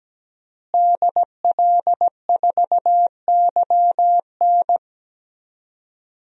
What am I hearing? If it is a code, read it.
Morse code, heard as a keyed tone, about 17 wpm, 700 Hz. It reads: DL4YN
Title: DXCC-100 Call Signs - 5 Characters - Repeated 2x - 17wpm